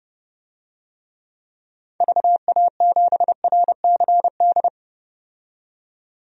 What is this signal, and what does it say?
Morse code, heard as a keyed tone, about 30 wpm, 700 Hz. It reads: VA7RCD
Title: DXCC-100 Call Signs - 6 Characters - 30wpm